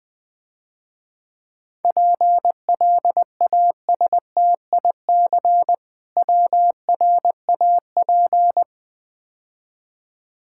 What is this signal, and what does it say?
Morse code, heard as a keyed tone, about 20 wpm, 700 Hz. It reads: PLASTIC WRAP
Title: Easy Words - Speed-Racing with Hints - 20wpm